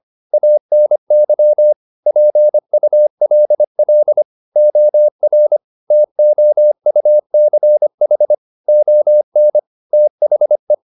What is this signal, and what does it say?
Morse code, heard as a keyed tone, about 25 wpm, 600 Hz. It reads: ANY PULL OR TOUCH ON THE
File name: the old man and the sea part 1 25wpm